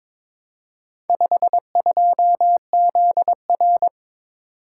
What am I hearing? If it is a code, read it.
Morse code, heard as a keyed tone, about 23 wpm, 700 Hz. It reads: 52ZR
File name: practice / 22wpm / Four Letter-Number Combinations - 22wpm